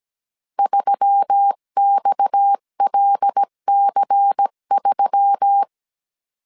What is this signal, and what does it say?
Morse code, heard as a keyed tone, about 17 wpm, 800 Hz. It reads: 3XLC3
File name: CW / DET / 1 6 1500C S C 16wpm 2733